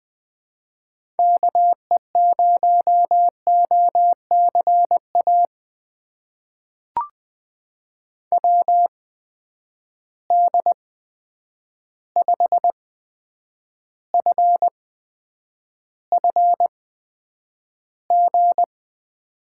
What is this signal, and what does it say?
Morse code, heard as a keyed tone, about 20 wpm, spaced as for 9 wpm, 700 Hz. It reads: KE0OCA WD5FFG
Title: USA Call Signs - 6 Characters - 8x Character Spacing - 20wpm